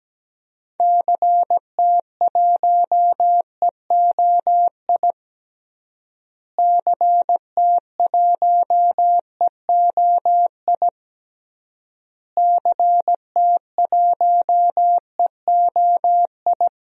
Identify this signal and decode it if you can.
Morse code, heard as a keyed tone, about 17 wpm, 700 Hz. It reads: CT1EOI CT1EOI CT1EOI
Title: DXCC-100 Call Signs - 6 Characters - Repeated 3x - 17wpm